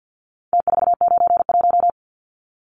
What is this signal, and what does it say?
Morse code, heard as a keyed tone, about 48 wpm, 700 Hz. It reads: T491
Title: Numbers - Contest Style - 4-Digit 50wpm